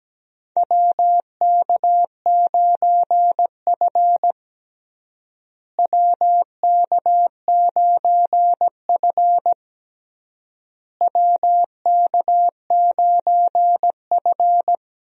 Morse code, heard as a keyed tone, about 17 wpm, 700 Hz. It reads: WK9F WK9F WK9F
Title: USA Call Signs - 4 Characters - Repeated 3x - 17wpm